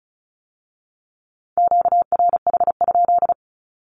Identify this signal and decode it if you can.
Morse code, heard as a keyed tone, about 36 wpm, 700 Hz. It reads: QRH?
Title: Q-Codes with Spoken Meaning - 35wpm